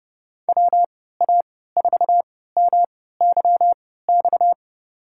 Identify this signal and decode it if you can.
Morse code, heard as a keyed tone, about 30 wpm, spaced as for 17 wpm, 700 Hz. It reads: WA4MYX
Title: USA Call Signs - 6 Characters - 3x Character Spacing - 30wpm